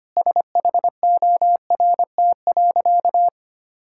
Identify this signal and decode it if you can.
Morse code, heard as a keyed tone, about 25 wpm, 700 Hz. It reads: SHORT.
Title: Sentences from Top 1600 Words 25wpm